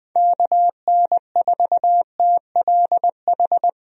Morse code, heard as a keyed tone, about 20 wpm, 700 Hz. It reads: KN4TLH
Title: USA Call Signs - 6 Characters - Repeated 3x - 20wpm